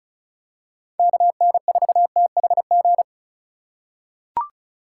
Morse code, heard as a keyed tone, about 36 wpm, 700 Hz. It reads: KN4THG
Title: USA Call Signs - 6 Characters - 3x Character Spacing - 35wpm